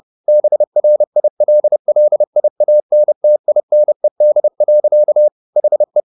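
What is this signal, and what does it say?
Morse code, heard as a keyed tone, about 30 wpm, 600 Hz. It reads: BRILLIANTINED. HE
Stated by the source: Morse code at 30 wpm, the novel Never by Ken Follett in English, from CW med Gnister.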